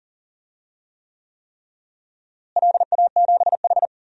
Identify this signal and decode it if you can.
Morse code, heard as a keyed tone, about 40 wpm, 700 Hz. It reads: LA7H
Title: DXCC-100 Call Signs - 4 Characters - Repeated 3x - 40wpm